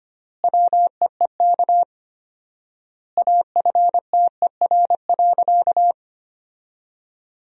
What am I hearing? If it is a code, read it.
Morse code, heard as a keyed tone, about 25 wpm, 700 Hz. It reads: WEEK AFTER.
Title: 4x Word-Spacing - Sentences from Top 500 Words - 25wpm